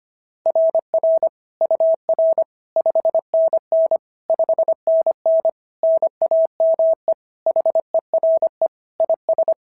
Morse code, heard as a keyed tone, about 22 wpm, 650 Hz. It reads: RR UR 5NN 5NN NAME HERE IS
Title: SOTA QSO Practice 25wpm